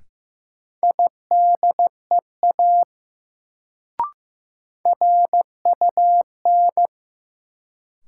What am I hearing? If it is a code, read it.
Morse code, heard as a keyed tone, about 15 wpm, 700 Hz. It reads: IDEA RUN
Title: Sentences from Top 300 Words - New Word Drill - 15wpm